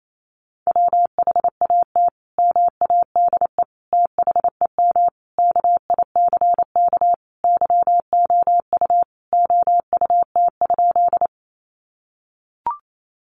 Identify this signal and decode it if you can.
Morse code, heard as a keyed tone, about 28 wpm, 700 Hz. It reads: WHAT MADE THEM KICK YOU OUT?
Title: Sentences from Top 1300 Words 28wpm